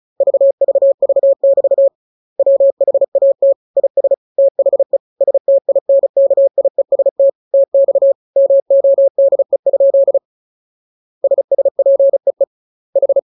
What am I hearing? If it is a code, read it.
Morse code, heard as a keyed tone, about 34 wpm, 550 Hz. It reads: VVV<BT> WHAT IS THE STINKIEST TX MODE? SSPEE H
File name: jokes / joke 27 35 0000